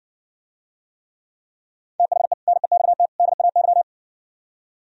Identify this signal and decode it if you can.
Morse code, heard as a keyed tone, about 68 wpm, 700 Hz. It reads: THE NEXT DAY